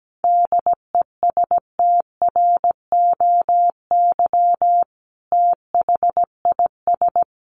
Morse code, heard as a keyed tone, about 17 wpm, 700 Hz. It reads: DESTROY THIS
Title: A Sound Of Thunder 17wpm